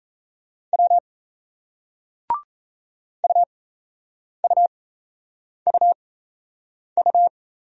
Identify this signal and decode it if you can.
Morse code, heard as a keyed tone, about 37 wpm, spaced as for 8 wpm, 700 Hz. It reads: W UUUU